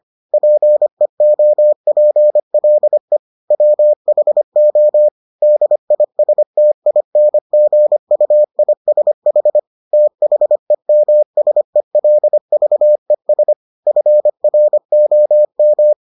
Morse code, heard as a keyed tone, about 25 wpm, 600 Hz. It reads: PEOPLE WHO DISTINGUISH THEMSELVES FROM